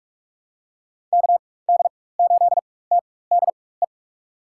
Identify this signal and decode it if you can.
Morse code, heard as a keyed tone, about 43 wpm, spaced as for 21 wpm, 700 Hz. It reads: KD8TDE
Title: USA Call Signs - 4x Character Spacing - 45wpm